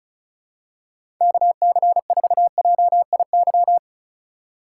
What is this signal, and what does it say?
Morse code, heard as a keyed tone, about 36 wpm, 700 Hz. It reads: KC4JIY
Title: USA Call Signs - 6 Characters - 3x Character Spacing - 35wpm